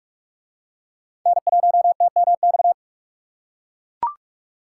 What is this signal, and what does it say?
Morse code, heard as a keyed tone, about 41 wpm, 700 Hz. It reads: N1TMX